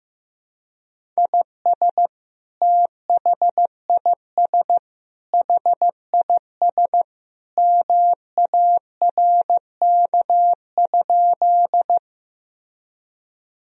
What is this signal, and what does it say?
Morse code, heard as a keyed tone, about 15 wpm, 700 Hz. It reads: IS THIS HIS MARK?
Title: Sentences from Top 1000 Words - Review Part 1 - 15wpm